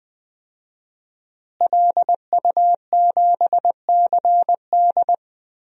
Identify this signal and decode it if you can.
Morse code, heard as a keyed tone, about 20 wpm, 700 Hz. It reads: LU7CD
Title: DXCC-100 Call Signs - 5 Characters - Repeated 3x - 20wpm